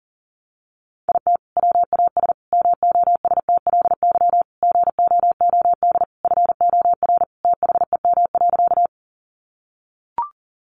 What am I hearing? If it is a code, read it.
Morse code, heard as a keyed tone, about 40 wpm, 700 Hz. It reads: IT WAS MOSTLY GOOD FOR THEM.
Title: Sentences from Top 1300 Words 40wpm